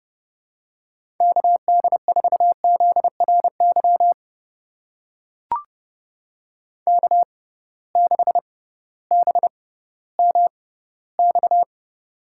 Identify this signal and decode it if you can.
Morse code, heard as a keyed tone, about 31 wpm, spaced as for 16 wpm, 700 Hz. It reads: KD4ZRY K6BMX
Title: USA Call Signs - 6x Character Spacing - 30wpm